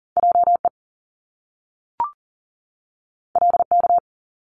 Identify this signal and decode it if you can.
Morse code, heard as a keyed tone, about 40 wpm, 700 Hz. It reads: JE LK
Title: Top Two Letter Combinations in Words 40wpm